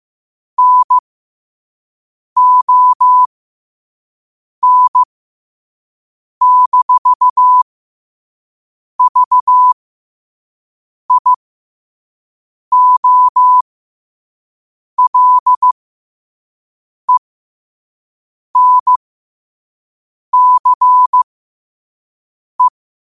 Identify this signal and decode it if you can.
Morse code, heard as a keyed tone, about 15 wpm, spaced as for 5 wpm, 1000 Hz. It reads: NON-VIOLENCE
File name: qotdfiles / 5wpm